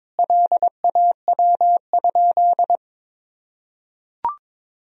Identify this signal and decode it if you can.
Morse code, heard as a keyed tone, about 22 wpm, 700 Hz. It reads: LAW?
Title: Sentences from Top 1000 Words - Review Part 4 - 22wpm